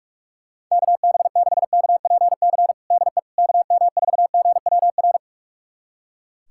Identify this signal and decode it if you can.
Morse code, heard as a keyed tone, about 44 wpm, 700 Hz. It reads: KB6KPC DE KM4GWR
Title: USA Callsign DE Callsign - 6 Characters - 45wpm